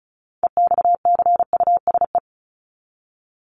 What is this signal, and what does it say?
Morse code, heard as a keyed tone, about 34 wpm, 700 Hz. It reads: EXCUSE
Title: Top 501-750 Words 35wpm